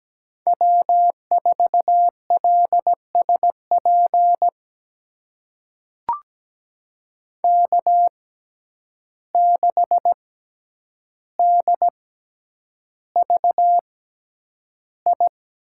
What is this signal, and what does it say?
Morse code, heard as a keyed tone, about 17 wpm, spaced as for 9 wpm, 700 Hz. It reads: W4LSP K6DVI
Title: USA Call Signs - 5 Characters - 6x Character Spacing - 17wpm